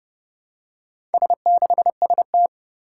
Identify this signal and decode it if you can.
Morse code, heard as a keyed tone, about 28 wpm, 700 Hz. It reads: S6ST
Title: Four Challenging Letter-Number Combinations 30wpm